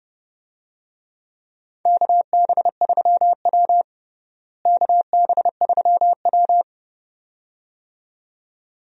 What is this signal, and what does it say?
Morse code, heard as a keyed tone, about 32 wpm, 700 Hz. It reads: KB3W KB3W
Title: USA Call Signs - 4 Characters - Repeated 2x - 30wpm